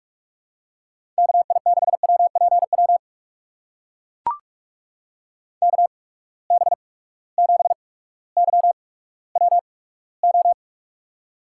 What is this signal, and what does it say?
Morse code, heard as a keyed tone, about 42 wpm, spaced as for 19 wpm, 700 Hz. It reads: KI6WPW KB7YWO
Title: USA Call Signs - 6 Characters - 8x Character Spacing - 45wpm